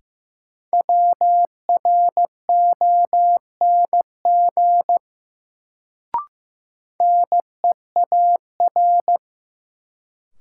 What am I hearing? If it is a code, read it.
Morse code, heard as a keyed tone, about 15 wpm, 700 Hz. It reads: WRONG NEAR
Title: Sets of 5 Words Encore - New Word Drill - 15wpm